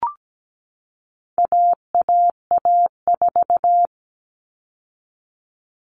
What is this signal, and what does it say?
Morse code, heard as a keyed tone, about 17 wpm, 700 Hz. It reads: AAA4
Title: All Cut Numbers - 4-Digit 17wpm